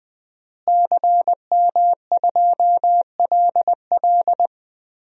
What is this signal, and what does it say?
Morse code, heard as a keyed tone, about 20 wpm, 700 Hz. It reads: CM2LL
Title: DXCC-100 Call Signs - 5 Characters - Repeated 2x - 20wpm